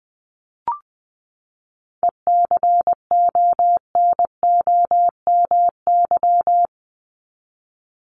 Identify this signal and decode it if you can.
Morse code, heard as a keyed tone, about 19 wpm, 700 Hz. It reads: ECONOMY
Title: Top 501-750 Words 20wpm